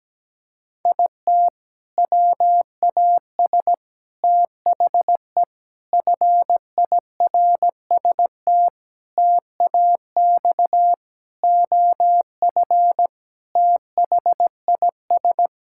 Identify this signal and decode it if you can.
Morse code, heard as a keyed tone, about 17 wpm, 700 Hz. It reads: IT WAS THE FIRST TAX OF THIS
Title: Sentences from Top 1000 Words - Review Part 5 - 17wpm